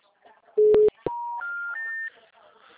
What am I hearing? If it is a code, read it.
Special information tone, with speech.
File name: out- -2009- - - .47665